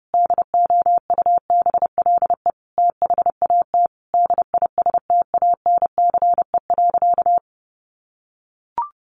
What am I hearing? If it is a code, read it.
Morse code, heard as a keyed tone, about 30 wpm, 700 Hz. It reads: DOUBLE THAT DISTANCE.
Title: Sentences from Top 1400 Words 30wpm